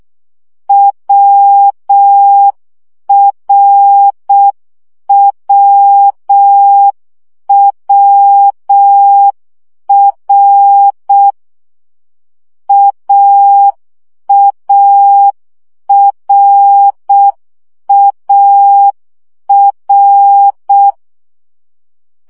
Morse code, heard as a keyed tone, about 6 wpm, 800 Hz. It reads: WRWWR AARAR
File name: CW gyakorlas a r w 5 baud